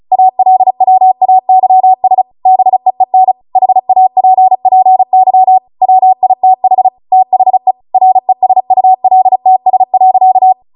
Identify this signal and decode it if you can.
Morse code, heard as a keyed tone, about 36 wpm, 750 Hz. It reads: ALWAYS BEEN HAPPY WITH THE RESULTS.
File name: Morse / 35WPM